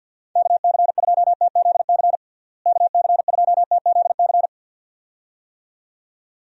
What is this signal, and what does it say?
Morse code, heard as a keyed tone, about 48 wpm, 700 Hz. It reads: KC2TZC KC2TZC
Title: USA Call Signs - 6 Characters - Repeated 2x - 50wpm